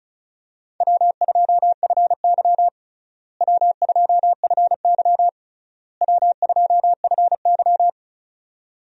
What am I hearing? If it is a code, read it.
Morse code, heard as a keyed tone, about 34 wpm, 700 Hz. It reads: W2FY W2FY W2FY
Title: USA Call Signs - 4 Characters - Repeated 3x - 35wpm